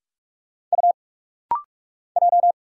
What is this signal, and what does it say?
Morse code, heard as a keyed tone, about 44 wpm, 690 Hz. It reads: U J
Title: Single Letter-Number - Rapid-Fire 45wpm